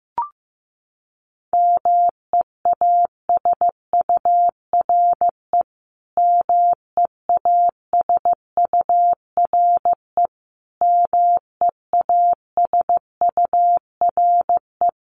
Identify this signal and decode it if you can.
Morse code, heard as a keyed tone, about 15 wpm, 700 Hz. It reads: MEASURE MEASURE MEASURE
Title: Top 500 Words Repeated 15wpm